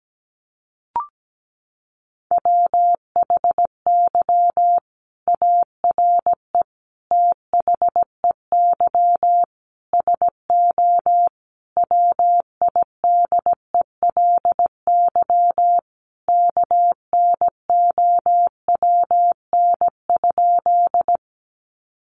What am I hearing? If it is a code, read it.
Morse code, heard as a keyed tone, about 17 wpm, 700 Hz. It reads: WHY ARE THEY SO WIDELY KNOWN?
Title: Sentences from Top 1200 Words 17wpm